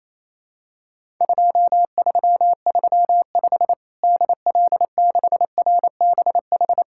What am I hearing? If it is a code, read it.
Morse code, heard as a keyed tone, about 28 wpm, 700 Hz. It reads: 2335 DL6RBH
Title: Worked All Europe DX Contest - QTC Exchange - 28wpm